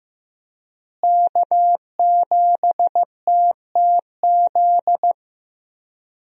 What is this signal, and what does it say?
Morse code, heard as a keyed tone, about 15 wpm, 700 Hz. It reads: K7TTZ
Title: USA Call Signs - 5 Characters - Repeated 3x - 15wpm